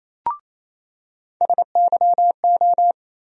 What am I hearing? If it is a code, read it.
Morse code, heard as a keyed tone, about 28 wpm, 700 Hz. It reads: SYO
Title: USA Call Sign Postfixes - 3 Letters 28wpm